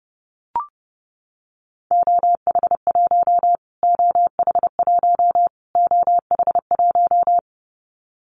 Morse code, heard as a keyed tone, about 28 wpm, 700 Hz. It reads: OH1 OH1 OH1
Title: Callsign Prefixes for Top 25 DXCC - Repeated 3x - 30wpm